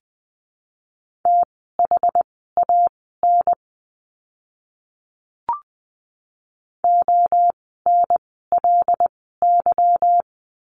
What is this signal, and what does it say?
Morse code, heard as a keyed tone, about 20 wpm, spaced as for 15 wpm, 700 Hz. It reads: THAN ONLY
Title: 237 Lesson 38f - Letter q - Words 20wpm